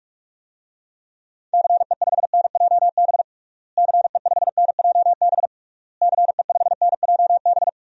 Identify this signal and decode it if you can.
Morse code, heard as a keyed tone, about 46 wpm, 700 Hz. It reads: CE5NJB CE5NJB CE5NJB
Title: DXCC-100 Call Signs - 6 Characters - Repeated 3x - 45wpm